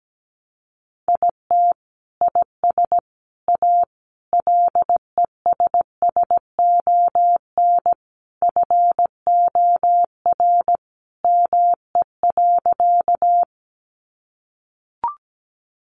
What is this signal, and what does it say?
Morse code, heard as a keyed tone, about 17 wpm, 700 Hz. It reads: IT IS A LESSON FOR ME.
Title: Sentences from Top 1200 Words 17wpm